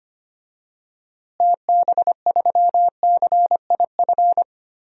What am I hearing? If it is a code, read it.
Morse code, heard as a keyed tone, about 25 wpm, 700 Hz. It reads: TB3CIF
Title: DXCC-100 Call Signs - 6 Characters - Repeated 2x - 25wpm